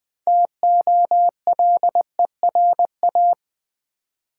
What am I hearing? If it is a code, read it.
Morse code, heard as a keyed tone, about 20 wpm, 700 Hz. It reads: TOLERA
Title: Build A Word - 3 Letters At A Time - 8-Letter Words - 20wpm